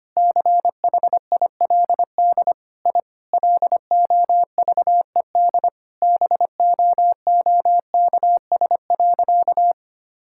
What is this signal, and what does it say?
Morse code, heard as a keyed tone, about 25 wpm, 700 Hz. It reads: CHILD I LOVED BOOKS.
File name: practice / 25wpm / Sentences with 3-Word Phrases - 25wpm